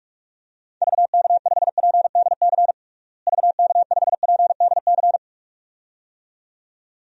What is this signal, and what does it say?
Morse code, heard as a keyed tone, about 46 wpm, 700 Hz. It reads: VK5PDC VK5PDC
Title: DXCC-100 Call Signs - 6 Characters - Repeated 2x - 45wpm